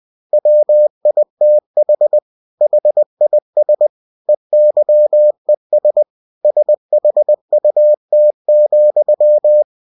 Morse code, heard as a keyed tone, about 20 wpm, 600 Hz. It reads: WITH HIS EYES SHUT,
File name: cw-books / winnie-the-pooh / winnie-20wpm-page-0031